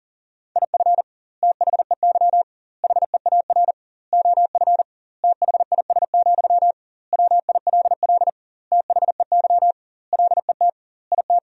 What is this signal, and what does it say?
Morse code, heard as a keyed tone, about 40 wpm, 700 Hz. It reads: IF THEY HEAR OF THIS, WILL THEY LET IT
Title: 2x Word-Spacing - Sentences from Top 400 Words - 40wpm